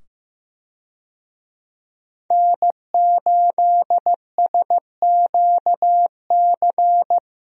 Morse code, heard as a keyed tone, about 15 wpm, 700 Hz. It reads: N8SQC
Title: USA Call Signs - 5 Characters - 15wpm